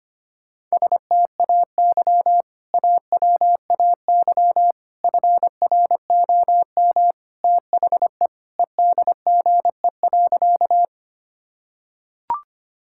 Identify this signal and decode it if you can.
Morse code, heard as a keyed tone, about 25 wpm, 700 Hz. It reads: STAY AWAY FROM THE EDGE.
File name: practice / 25wpm / Sentences with 3-Word Phrases - 25wpm